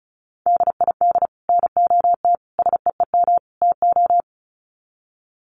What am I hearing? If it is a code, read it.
Morse code, heard as a keyed tone, about 34 wpm, 700 Hz. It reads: DID NOT SEEM TO
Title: Sets of 4 Words 35wpm